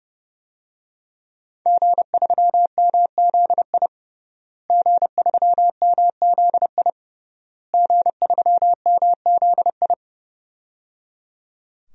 Morse code, heard as a keyed tone, about 28 wpm, 700 Hz. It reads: G3MZI G3MZI G3MZI
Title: DXCC-100 Call Signs - 5 Characters - Repeated 3x - 30wpm